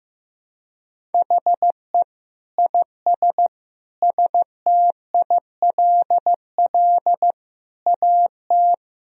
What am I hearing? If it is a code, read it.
Morse code, heard as a keyed tone, about 15 wpm, 700 Hz. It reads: HE IS STILL AT
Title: Sentences from Top 1000 Words - Review Part 1 - 15wpm